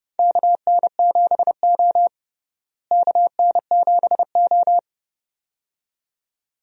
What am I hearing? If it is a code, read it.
Morse code, heard as a keyed tone, about 30 wpm, 700 Hz. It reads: KN7O KN7O
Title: USA Call Signs - 4 Characters - Repeated 2x - 30wpm